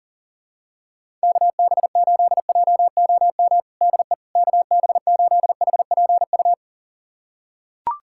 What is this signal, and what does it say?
Morse code, heard as a keyed tone, about 40 wpm, 700 Hz. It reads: KB8JOM DE KB8HPU
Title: USA Callsign DE Callsign - 6 Characters - 40wpm